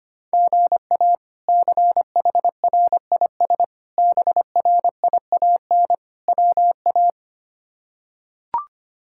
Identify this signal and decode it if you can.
Morse code, heard as a keyed tone, about 25 wpm, 700 Hz. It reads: GA CHRIS BRIAN WA
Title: K1USN SST QSO Practice - 25wpm